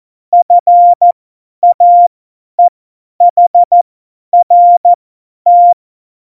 Morse code, heard as a keyed tone, about 14 wpm, spaced as for 10 wpm, 700 Hz. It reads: FAEHRT
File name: Morsekurs / Witze / Witz 12. Februar 2019, 50 F70